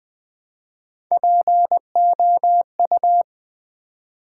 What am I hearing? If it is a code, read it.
Morse code, heard as a keyed tone, about 20 wpm, 700 Hz. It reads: POU
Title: Build A Word - 1 Letter At A Time - 4-Letter Words - 20wpm